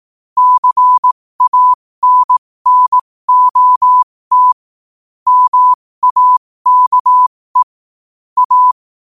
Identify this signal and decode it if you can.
Morse code, heard as a keyed tone, about 18 wpm, spaced as for 15 wpm, 1000 Hz. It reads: CANNOT MAKE A
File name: qotdfiles / 15wpm